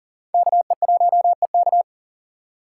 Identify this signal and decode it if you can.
Morse code, heard as a keyed tone, about 37 wpm, 700 Hz. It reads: KE1EK
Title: USA Call Signs - 5 Characters - 3x Character Spacing - 40wpm